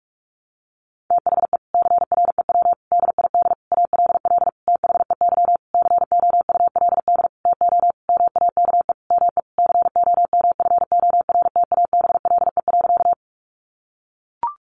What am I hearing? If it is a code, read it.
Morse code, heard as a keyed tone, about 44 wpm, 700 Hz. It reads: THE CREW DID ALL THEY COULD TO MAKE ME COMFORTABLE.
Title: Sentences from Top 1500 Words 45wpm